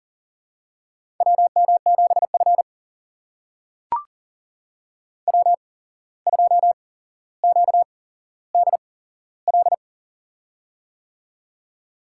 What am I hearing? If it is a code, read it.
Morse code, heard as a keyed tone, about 40 wpm, spaced as for 16 wpm, 700 Hz. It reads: WM7F W2QDL
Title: USA Call Signs - 8x Character Spacing - 40wpm